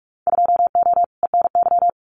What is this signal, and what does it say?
Morse code, heard as a keyed tone, about 46 wpm, 700 Hz. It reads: 2O ENY
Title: Winter Field Day Exchanges 45wpm